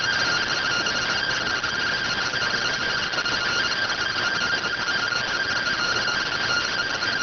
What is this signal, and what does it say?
Two-tone teleprinter signal, about 170 Hz shift.